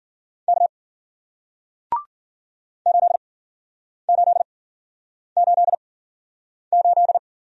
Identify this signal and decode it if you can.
Morse code, heard as a keyed tone, about 53 wpm, spaced as for 9 wpm, 700 Hz. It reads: K 8888